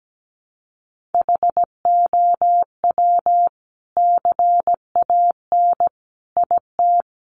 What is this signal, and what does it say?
Morse code, heard as a keyed tone, about 17 wpm, 700 Hz. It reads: HOW CAN IT
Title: Sentences from Top 1400 Words 17wpm